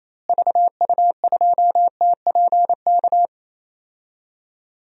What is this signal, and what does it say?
Morse code, heard as a keyed tone, about 28 wpm, 700 Hz. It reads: VU2TPK
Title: DXCC-100 Call Signs - 6 Characters - Repeated 3x - 28wpm